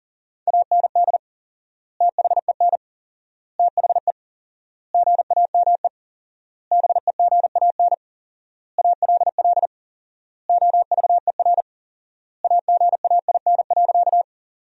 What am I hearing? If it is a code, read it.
Morse code, heard as a keyed tone, about 40 wpm, 700 Hz. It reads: AND THEN THE GAME BEGAN ALL OVER AGAIN.
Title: 4x Word-Spacing - Sentences from Top 500 Words - 40wpm